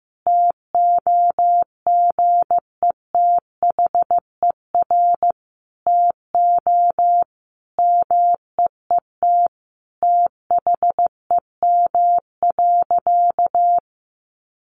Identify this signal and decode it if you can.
Morse code, heard as a keyed tone, about 15 wpm, 700 Hz. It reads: TOGETHER TO MEET THEM.
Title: Sentences from Top 1100 Words 15wpm